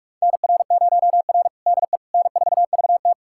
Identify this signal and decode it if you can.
Morse code, heard as a keyed tone, about 44 wpm, 690 Hz. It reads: NR0R DE N4VT
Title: USA Callsign DE Callsign - 4 Characters - 45wpm